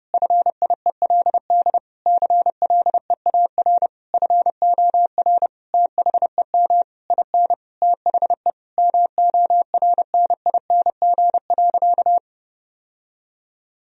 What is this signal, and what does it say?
Morse code, heard as a keyed tone, about 30 wpm, 700 Hz. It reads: FIELD CLEAR FOR THEM IN THE MORNING.
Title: Sentences from Top 1000 Words - Review Part 2 - 30wpm